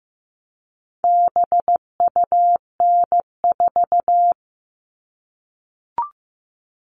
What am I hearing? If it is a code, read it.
Morse code, heard as a keyed tone, about 15 wpm, 700 Hz. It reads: BUN4
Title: All Cut Numbers - 4-Digit 15wpm